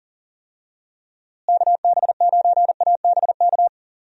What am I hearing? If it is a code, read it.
Morse code, heard as a keyed tone, about 40 wpm, 700 Hz. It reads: KB9ABK
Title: USA Call Signs - 6 Characters - Repeated 3x - 40wpm